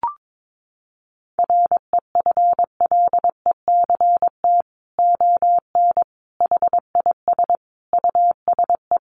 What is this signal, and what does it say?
Morse code, heard as a keyed tone, about 22 wpm, 700 Hz. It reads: REFLECT ON HIS USE
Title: Sentences from Top 1500 Words 22wpm